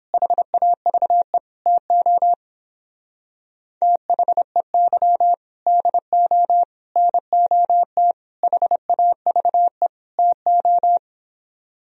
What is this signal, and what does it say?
Morse code, heard as a keyed tone, about 29 wpm, 700 Hz. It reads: HAVE TO THEY DO NOT HAVE TO